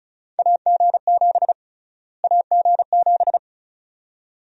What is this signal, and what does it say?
Morse code, heard as a keyed tone, about 36 wpm, 700 Hz. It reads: AG7 AG7
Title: USA Call Sign Prefixes - Repeated 2x - 35wpm